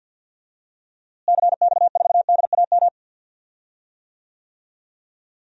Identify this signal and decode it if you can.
Morse code, heard as a keyed tone, about 53 wpm, 700 Hz. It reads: CX4DAM
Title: DXCC-100 Call Signs - 6 Characters - Repeated 2x - 50wpm